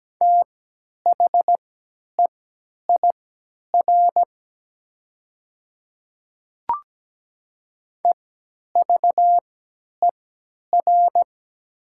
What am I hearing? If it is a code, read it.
Morse code, heard as a keyed tone, about 17 wpm, spaced as for 10 wpm, 700 Hz. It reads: THEIR EVER
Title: 3x Character Spacing - Top 500 Words - 17wpm